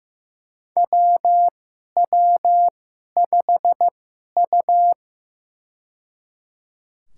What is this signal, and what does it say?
Morse code, heard as a keyed tone, about 15 wpm, spaced as for 11 wpm, 700 Hz. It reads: WW5U
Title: USA Call Signs - 2x Character Spacing - 15wpm